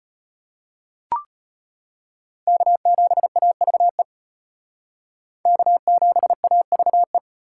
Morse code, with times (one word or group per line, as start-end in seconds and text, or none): 2.5-4.0 K7AVE
5.5-7.2 K7AVE